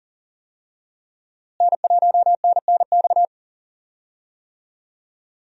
Morse code, heard as a keyed tone, about 40 wpm, 700 Hz. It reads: N1NNX
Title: USA Call Signs - 5 Characters - Repeated 3x - 40wpm